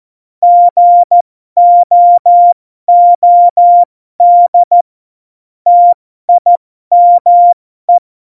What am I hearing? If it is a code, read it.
Morse code, heard as a keyed tone, about 14 wpm, spaced as for 12 wpm, 700 Hz. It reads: GOOD TIME
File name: Morsekurs / Witze